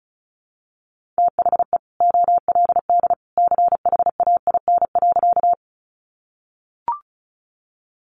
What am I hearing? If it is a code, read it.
Morse code, heard as a keyed tone, about 34 wpm, 700 Hz. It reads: THE OLD CHAIN.
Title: Sentences from Top 1500 Words 35wpm